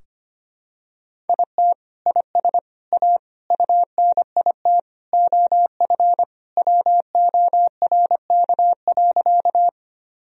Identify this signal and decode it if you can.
Morse code, heard as a keyed tone, about 25 wpm, 700 Hz. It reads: IT IS A UNIT OF WORK.
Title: Sentences from Top 1000 Words - Review Part 2 - 25wpm